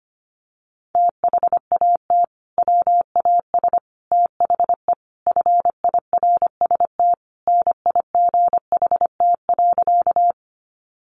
Morse code, heard as a keyed tone, about 25 wpm, 700 Hz. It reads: THAT WAS THE FIRST NIGHT.
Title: Sentences from Top 300 Words 25wpm